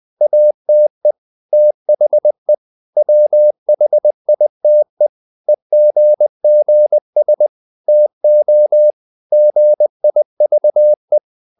Morse code, heard as a keyed tone, about 20 wpm, 600 Hz. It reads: ATE THE WHITE EGGS TO GIVE
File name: the old man and the sea / the old man and the sea part 1 20wpm